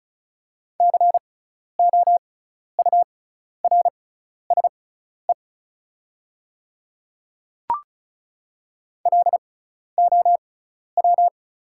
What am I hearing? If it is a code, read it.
Morse code, heard as a keyed tone, about 36 wpm, spaced as for 12 wpm, 700 Hz. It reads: COURSE LOW
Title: 6x Character Spacing - Top 500 Words - 35wpm